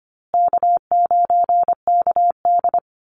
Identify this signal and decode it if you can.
Morse code, heard as a keyed tone, about 25 wpm, 700 Hz. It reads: K9KD
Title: USA Call Signs Repeated 3x 25wpm